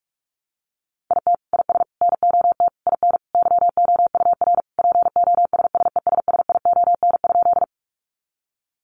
Morse code, heard as a keyed tone, about 44 wpm, 700 Hz. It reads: IT IS NOT IN YOUR POSSESSION?
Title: Sentences from Top 1400 Words 45wpm